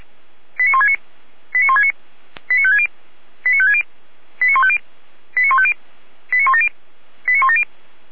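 Five-tone selective call (ZVEI), heard as quick runs of five tones, tones of about 70 ms. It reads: 89169 89169 89560 89560 89150 89150 89170 89170
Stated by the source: ZVEI five-tone call 89223, received on an SDR, from the Feuerwehr Orlamünde (fire brigade) audio server.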